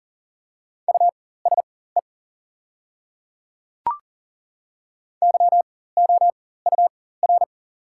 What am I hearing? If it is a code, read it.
Morse code, heard as a keyed tone, about 40 wpm, spaced as for 19 wpm, 690 Hz. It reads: USE YOUR
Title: 4x Character Spacing - Top 100 Words - 40wpm